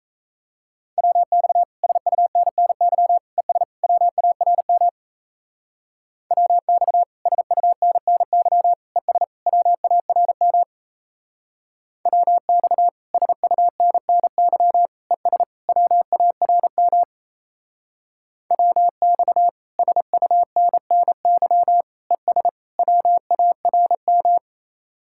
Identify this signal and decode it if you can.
Morse code, heard as a keyed tone, about 35 wpm, 700 Hz. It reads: WX SUNNY ES WARM WX SUNNY ES WARM WX SUNNY ES WARM WX SUNNY ES WARM
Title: Speed-Racing Common QSO Phrases - 28wpm